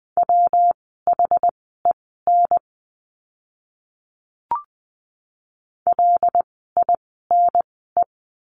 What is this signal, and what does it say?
Morse code, heard as a keyed tone, about 20 wpm, spaced as for 15 wpm, 700 Hz. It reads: WHEN LINE
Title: 115 Lesson 19f - Letter w - Words 20wpm